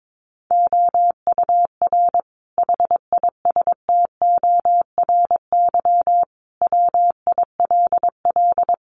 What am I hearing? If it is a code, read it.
Morse code, heard as a keyed tone, about 22 wpm, 700 Hz. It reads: OUR HISTORY WILL
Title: Sentences from Top 1500 Words 22wpm